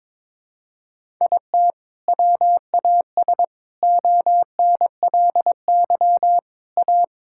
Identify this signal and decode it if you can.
Morse code, heard as a keyed tone, about 22 wpm, 700 Hz. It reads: IT WAS ONLY A
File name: practice / 22wpm / Sentences with 8-Word Phrases - 22wpm